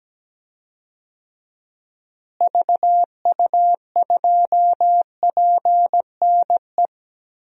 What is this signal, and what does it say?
Morse code, heard as a keyed tone, about 17 wpm, 700 Hz. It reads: VU2PNE
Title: DXCC-100 Call Signs - 6 Characters - Repeated 2x - 17wpm